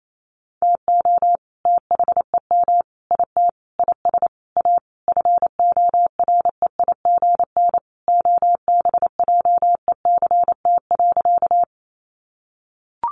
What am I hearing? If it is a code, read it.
Morse code, heard as a keyed tone, about 28 wpm, 700 Hz. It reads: TO THEM IT IS A FOREIGN OBJECT.
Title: Sentences from Top 1400 Words 28wpm